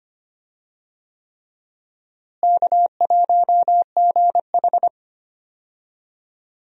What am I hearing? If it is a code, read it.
Morse code, heard as a keyed tone, about 25 wpm, 700 Hz. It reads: K1GH
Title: USA Call Signs - 4 Characters - Repeated 3x - 25wpm